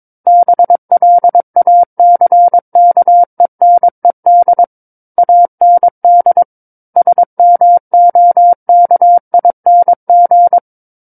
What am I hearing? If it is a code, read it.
Morse code, heard as a keyed tone, about 22 wpm, 700 Hz. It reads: BLACKENED AND SMOKING
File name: Book One / 13-Chapter 13 - 22 WPM